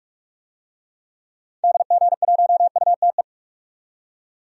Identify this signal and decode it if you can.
Morse code, heard as a keyed tone, about 47 wpm, 700 Hz. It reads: DG1UTE
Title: DXCC-100 Call Signs - 6 Characters - 45wpm